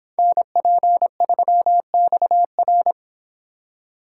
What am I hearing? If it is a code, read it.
Morse code, heard as a keyed tone, about 31 wpm, 700 Hz. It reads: NP3XR